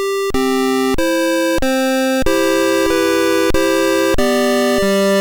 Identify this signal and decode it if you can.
There is no code.